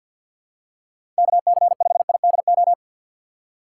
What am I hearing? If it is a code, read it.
Morse code, heard as a keyed tone, about 51 wpm, 700 Hz. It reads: KC5IDO
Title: USA Call Signs - 2x Character Spacing - 50wpm